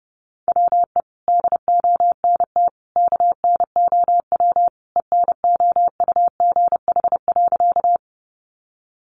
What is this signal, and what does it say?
Morse code, heard as a keyed tone, about 30 wpm, 700 Hz. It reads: WE DONT KNOW ENOUGH.
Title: Sentences from Top 400 Words 30wpm